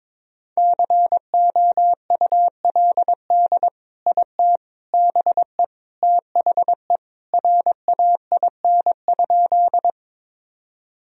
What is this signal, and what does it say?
Morse code, heard as a keyed tone, about 22 wpm, 700 Hz. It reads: COULD IT BE THE RAIN?
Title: Sentences from Top 1000 Words - Review Part 1 - 22wpm